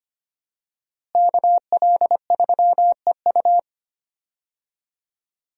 Morse code, heard as a keyed tone, about 25 wpm, 700 Hz. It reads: KL3EU
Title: USA Call Signs - 5 Characters - Repeated 3x - 25wpm